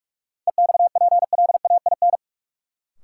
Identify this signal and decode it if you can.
Morse code, heard as a keyed tone, about 42 wpm, 700 Hz. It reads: EXPLAIN
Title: Sentences from Top 1100 Words - New Word Drill Alt - 45wpm